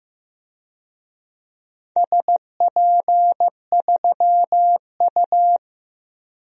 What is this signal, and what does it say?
Morse code, heard as a keyed tone, about 16 wpm, 700 Hz. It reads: SP3U
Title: DXCC-100 Call Signs - 4 Characters - 15wpm